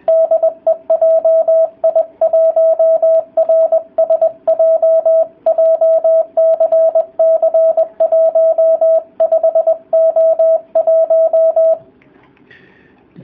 Morse code, heard as a keyed tone, about 20 wpm, 650 Hz, with radio noise. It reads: DE JI1RSJ JCC 15O1